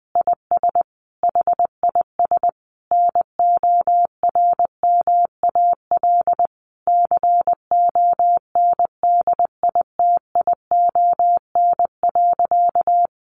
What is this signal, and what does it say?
Morse code, heard as a keyed tone, about 20 wpm, 700 Hz. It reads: IS HIS NORMAL CONDITION.
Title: Sentences from Top 1200 Words 20wpm